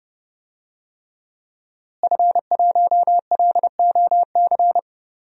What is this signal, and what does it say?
Morse code, heard as a keyed tone, about 30 wpm, 700 Hz. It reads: F1LOC
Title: DXCC-100 Call Signs - 5 Characters - Repeated 3x - 30wpm